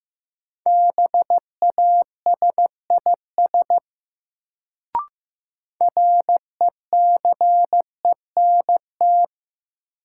Morse code, heard as a keyed tone, about 15 wpm, 700 Hz. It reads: BASIS RECENT
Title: Sets of 3 Words - New Word Drill - 15wpm